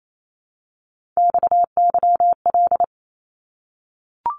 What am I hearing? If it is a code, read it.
Morse code, heard as a keyed tone, about 28 wpm, 700 Hz. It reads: XYL
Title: Abbreviations Repeated 3x - 28wpm